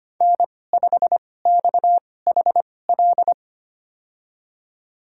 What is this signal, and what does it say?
Morse code, heard as a keyed tone, about 25 wpm, spaced as for 18 wpm, 700 Hz. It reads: N5XHL
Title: USA Call Signs - 5 Characters - 2x Character Spacing - 25wpm